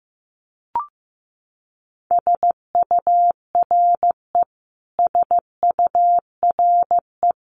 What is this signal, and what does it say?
Morse code, heard as a keyed tone, about 15 wpm, 700 Hz. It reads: SURE SURE
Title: Top 300 Words Repeated 15wpm